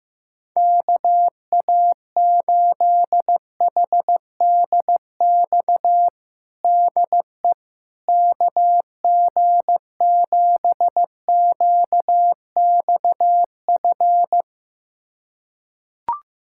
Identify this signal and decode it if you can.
Morse code, heard as a keyed tone, about 16 wpm, 700 Hz. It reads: KA8HDX DE KG7QXF
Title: USA Callsign DE Callsign - 6 Characters - 15wpm